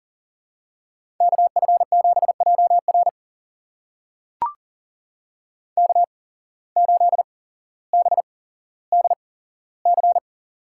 Morse code, heard as a keyed tone, about 40 wpm, spaced as for 17 wpm, 700 Hz. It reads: KF7JR K8BDC
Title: USA Call Signs - 5 Characters - 8x Character Spacing - 40wpm